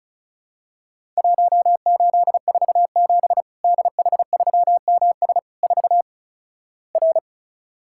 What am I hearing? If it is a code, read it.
Morse code, heard as a keyed tone, about 34 wpm, 700 Hz. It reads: 1847 DH3MS 4 R
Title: Worked All Europe DX Contest - QTC Exchange - 35wpm